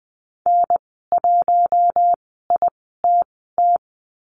Morse code, heard as a keyed tone, about 20 wpm, spaced as for 14 wpm, 700 Hz. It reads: N1ITT
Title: 042 Lesson 8g - Number 1 - Call Signs 20wpm